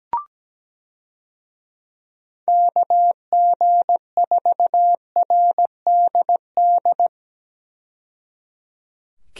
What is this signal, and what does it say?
Morse code, heard as a keyed tone, about 17 wpm, 700 Hz. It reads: KG4RDD
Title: USA Call Signs - 6 Characters - 17wpm